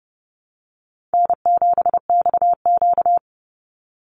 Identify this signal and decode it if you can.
Morse code, heard as a keyed tone, about 30 wpm, 700 Hz. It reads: N7XQ
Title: USA Call Signs 30wpm